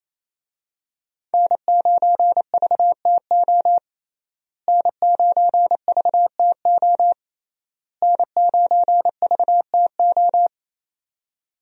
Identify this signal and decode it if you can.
Morse code, heard as a keyed tone, about 28 wpm, 700 Hz. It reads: N9VTO N9VTO N9VTO
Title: USA Call Signs - 5 Characters - Repeated 3x - 28wpm